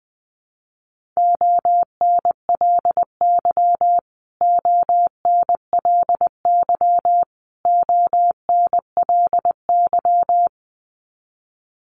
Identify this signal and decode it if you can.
Morse code, heard as a keyed tone, about 20 wpm, 700 Hz. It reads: ONLY ONLY ONLY
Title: Top 500 Words Repeated 20wpm